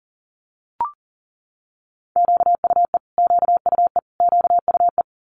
Morse code, heard as a keyed tone, about 37 wpm, 700 Hz. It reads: QUE QUE QUE
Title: Multi-letter Phonemes Repeated 3x 40wpm